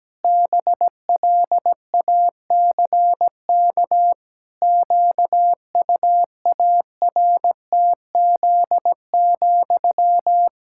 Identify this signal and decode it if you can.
Morse code, heard as a keyed tone, about 17 wpm, 700 Hz. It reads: BLACK QUARTZ,
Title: Pangram Challenge v2 - 17wpm